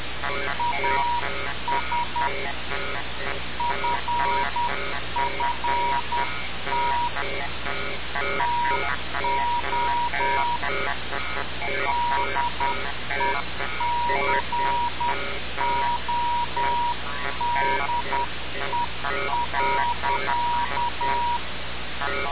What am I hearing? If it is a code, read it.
Morse code, heard as a keyed tone, about 10 wpm, 960 Hz. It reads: AS FFT TO DECODE.
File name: radiocap / REC z